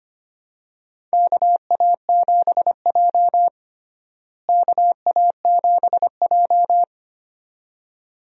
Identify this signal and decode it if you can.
Morse code, heard as a keyed tone, about 25 wpm, 700 Hz. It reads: KA7J KA7J
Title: USA Call Signs - 4 Characters - Repeated 2x - 25wpm